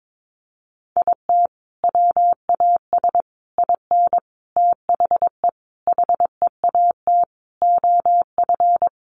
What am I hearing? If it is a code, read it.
Morse code, heard as a keyed tone, about 22 wpm, 700 Hz. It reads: IT WAS IN THE HEAT OF
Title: Sentences from Top 1300 Words 22wpm